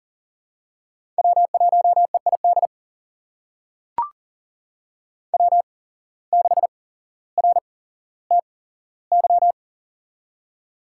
Morse code, heard as a keyed tone, about 40 wpm, spaced as for 17 wpm, 700 Hz. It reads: W1EID W6RTY
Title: USA Call Signs - 5 Characters - 8x Character Spacing - 40wpm